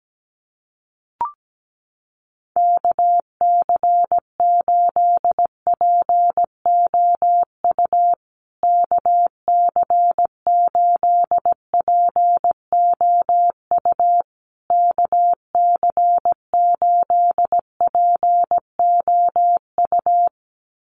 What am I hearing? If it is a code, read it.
Morse code, heard as a keyed tone, about 17 wpm, 700 Hz. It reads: KC8POU KC8POU KC8POU
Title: USA Call Signs Repeated 3x 17wpm